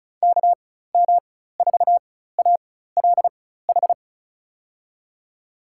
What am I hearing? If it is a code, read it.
Morse code, heard as a keyed tone, about 34 wpm, spaced as for 16 wpm, 700 Hz. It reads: KM4ALH
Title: USA Call Signs - 4x Character Spacing - 35wpm